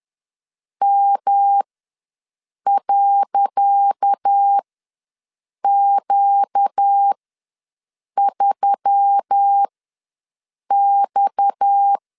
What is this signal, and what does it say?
Morse code, heard as a keyed tone, about 10 wpm, spaced as for 6 wpm, 800 Hz. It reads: M.Q3X
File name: LEZIONI / 10 SIGNS / 10D COMA ws 10wpm 4314